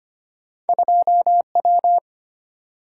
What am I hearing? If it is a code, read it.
Morse code, heard as a keyed tone, about 34 wpm, spaced as for 30 wpm, 700 Hz. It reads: 2W
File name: practice / 25wpm / Speed-Racing - Two Letter-Number Combinations - 25wpm